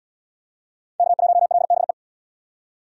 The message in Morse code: K1LBE

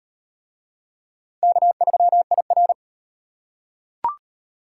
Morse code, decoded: K3IR

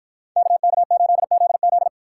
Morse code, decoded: KK8ZZ

